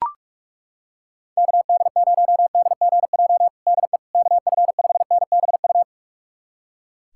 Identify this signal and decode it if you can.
Morse code, heard as a keyed tone, about 41 wpm, 700 Hz. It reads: KD0DGJ DE KF5NBU
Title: USA Callsign DE Callsign - 6 Characters - 45wpm